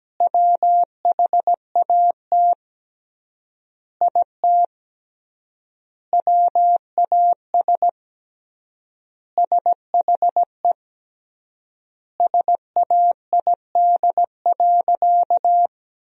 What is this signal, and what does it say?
Morse code, heard as a keyed tone, about 17 wpm, 700 Hz. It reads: WHAT IT WAS SHE SAID.
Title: 3x Word-Spacing - Sentences from Top 400 Words - 17wpm